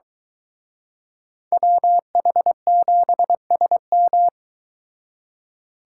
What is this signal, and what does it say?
Morse code, heard as a keyed tone, about 23 wpm, 700 Hz. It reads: WH7SM